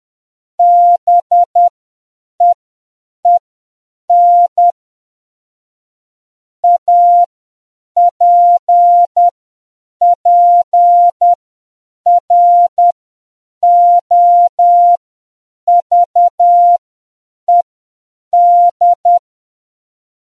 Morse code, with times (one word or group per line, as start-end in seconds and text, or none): 0.6-4.7 BEEN
6.6-19.2 APPROVED